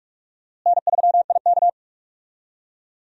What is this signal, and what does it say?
Morse code, heard as a keyed tone, about 46 wpm, 700 Hz. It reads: N3IK
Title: USA Call Signs - 4 Characters - Repeated 3x - 45wpm